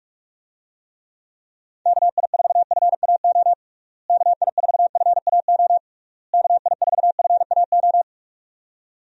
Morse code, heard as a keyed tone, about 47 wpm, 700 Hz. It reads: KI4FAO KI4FAO KI4FAO
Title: USA Call Signs - 6 Characters - Repeated 3x - 45wpm